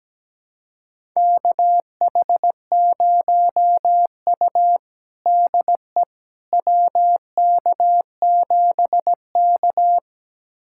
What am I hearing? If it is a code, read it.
Morse code, heard as a keyed tone, about 17 wpm, 700 Hz. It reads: KH0U DE WK7K
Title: USA Callsign DE Callsign - 4 Characters - 17wpm